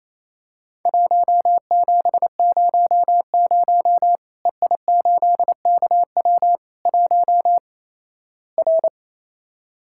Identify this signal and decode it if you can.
Morse code, heard as a keyed tone, about 28 wpm, 700 Hz. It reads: 1700 EI8KW 1 R
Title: Worked All Europe DX Contest - QTC Exchange - 28wpm